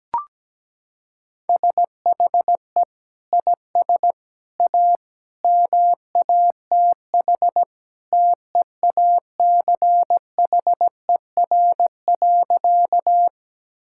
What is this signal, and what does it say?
Morse code, heard as a keyed tone, about 17 wpm, 700 Hz. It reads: SHE IS A MATH TEACHER.
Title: Sentences from Top 1000 Words - Review Part 6 - 17wpm